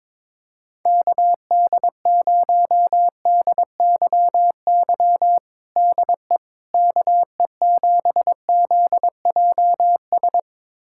Morse code, heard as a keyed tone, about 21 wpm, 700 Hz. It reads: KD0DYY DE KE7ZJS
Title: USA Callsign DE Callsign - 6 Characters - 22wpm